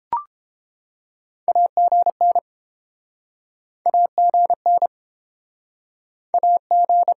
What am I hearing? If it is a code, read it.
Morse code, heard as a keyed tone, about 31 wpm, 700 Hz. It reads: AGN AGN AG